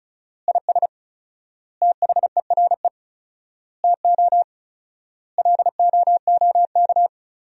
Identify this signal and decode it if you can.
Morse code, heard as a keyed tone, about 36 wpm, 700 Hz. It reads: IS THERE TO LOOK
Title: 4x Word-Spacing - Sentences from Top 100 Words - 35wpm